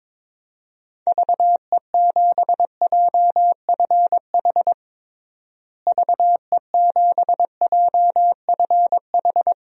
Morse code, heard as a keyed tone, about 22 wpm, 700 Hz. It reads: VE7JFH VE7JFH
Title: DXCC-100 Call Signs - 6 Characters - Repeated 2x - 22wpm